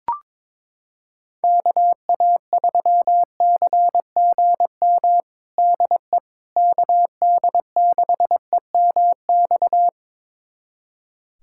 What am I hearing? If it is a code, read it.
Morse code, heard as a keyed tone, about 22 wpm, 700 Hz. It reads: KA3CGM DE KD6EMX
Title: USA Callsign DE Callsign 22wpm